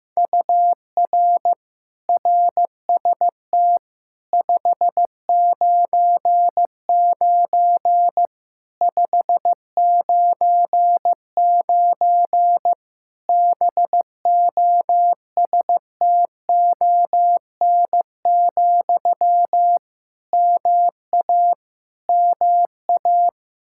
Morse code, heard as a keyed tone, about 15 wpm, 700 Hz. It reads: UR RST 599 599 BOSTON, MA MA
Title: Practice QSO - Exchange 1 - Beginning - 15wpm